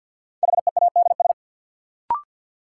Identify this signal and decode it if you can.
Morse code, heard as a keyed tone, about 49 wpm, 700 Hz. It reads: HEADS
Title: Sentences from Top 1000 Words - Review Part 3 - New Word Drill - 50wpm